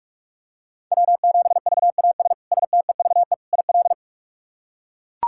Morse code, heard as a keyed tone, about 43 wpm, 700 Hz. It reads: W7VAS STEVE IL
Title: K1USN SST QSO Practice - 45wpm